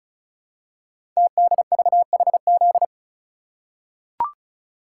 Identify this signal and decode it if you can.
Morse code, heard as a keyed tone, about 36 wpm, 700 Hz. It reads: TDVHZ
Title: Code Groups - 5 Letters - 35wpm